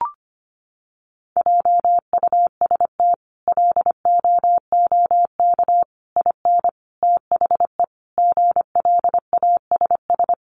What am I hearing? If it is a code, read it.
Morse code, heard as a keyed tone, about 25 wpm, 700 Hz. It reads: JUST LOOK IN THE GLASS
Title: Sentences from Top 1300 Words 25wpm